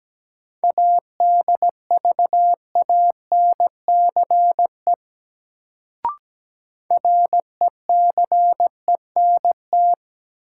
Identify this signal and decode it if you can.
Morse code, heard as a keyed tone, about 17 wpm, 700 Hz. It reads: ADVANCE RECENT
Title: Sentences from Top 1400 Words - New Word Drill Alt - 17wpm